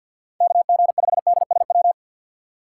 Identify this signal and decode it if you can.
Morse code, heard as a keyed tone, about 52 wpm, 700 Hz. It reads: KG5DSW
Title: USA Call Signs - 6 Characters - 4x Character Spacing - 50wpm